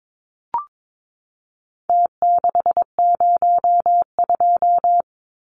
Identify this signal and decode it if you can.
Morse code, heard as a keyed tone, about 22 wpm, 700 Hz. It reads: T602
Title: Numbers - Contest Style - 4-Digit 22wpm